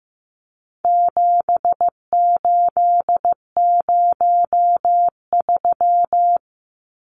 Morse code, heard as a keyed tone, about 15 wpm, 700 Hz. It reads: 7803
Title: Numbers - 4-Digit 15wpm